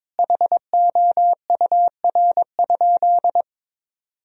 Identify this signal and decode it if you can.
Morse code, heard as a keyed tone, about 22 wpm, 700 Hz. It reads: HOUR?
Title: Sentences from Top 1000 Words - Review Part 4 - 22wpm